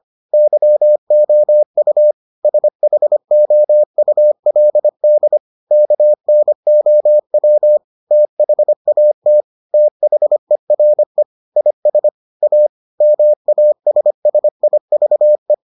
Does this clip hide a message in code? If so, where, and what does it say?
Yes: Morse code from 0.3 to 15.5 s, heard as a keyed tone, about 25 wpm, 600 Hz. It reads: YOU SHOULD KNOW THAT THERE IS A MASSIVE